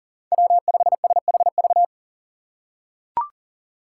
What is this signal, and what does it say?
Morse code, heard as a keyed tone, about 40 wpm, 700 Hz. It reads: W5SHV
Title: USA Call Signs - 5 Characters - 4x Character Spacing - 40wpm